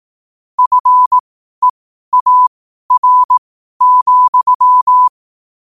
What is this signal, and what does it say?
Morse code, heard as a keyed tone, about 18 wpm, spaced as for 12 wpm, 1000 Hz. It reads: FEAR,
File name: qotdfiles / 12wpm